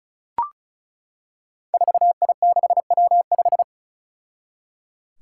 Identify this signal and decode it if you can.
Morse code, heard as a keyed tone, about 34 wpm, 700 Hz. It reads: 4I6W5
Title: Five Challenging Letter-Number Combinations 35wpm